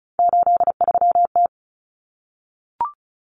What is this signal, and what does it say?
Morse code, heard as a keyed tone, about 34 wpm, 700 Hz. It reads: 83T
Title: Common Cut Numbers - 3-Digit 35wpm